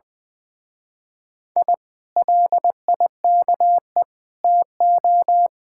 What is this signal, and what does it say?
Morse code, heard as a keyed tone, about 20 wpm, 700 Hz. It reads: I LIKE TO